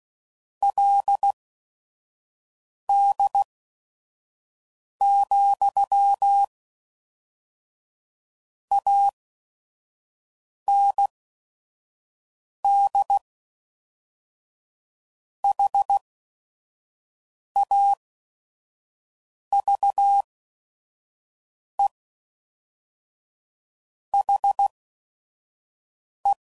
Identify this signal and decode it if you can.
Morse code, heard as a keyed tone, about 16 wpm, spaced as for 5 wpm, 800 Hz. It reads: LD, AND HAVE HE
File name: CW Code Groups @5WPM / QSOs / kc7haa